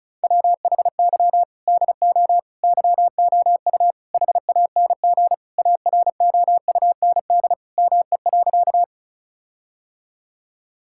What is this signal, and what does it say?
Morse code, heard as a keyed tone, about 36 wpm, 700 Hz. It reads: WHY DO YOU HANG AROUND ME.
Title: Sentences from Top 1600 Words 35wpm